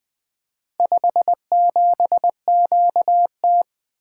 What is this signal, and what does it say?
Morse code, heard as a keyed tone, about 20 wpm, 700 Hz. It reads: 57QT
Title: Four Letter-Number Combinations - 20wpm